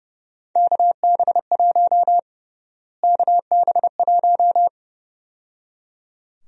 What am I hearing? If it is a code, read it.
Morse code, heard as a keyed tone, about 30 wpm, 700 Hz. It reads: KB1 KB1
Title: USA Call Sign Prefixes - Repeated 2x - 30wpm